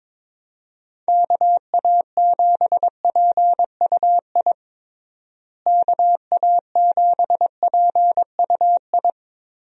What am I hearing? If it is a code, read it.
Morse code, heard as a keyed tone, about 22 wpm, 700 Hz. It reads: KA7PUI KA7PUI
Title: USA Call Signs - 6 Characters - Repeated 3x - 22wpm